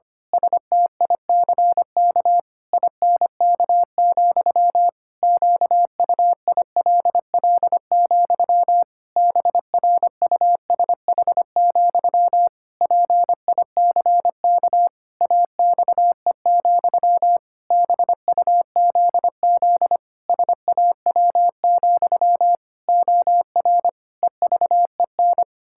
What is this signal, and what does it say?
Morse code, heard as a keyed tone, about 25 wpm, 700 Hz. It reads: STICK INK, QUILL, BRUSH, PICK AXE, BUZZ SAW, OR EVEN
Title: Pangram Challenge v2 - 25wpm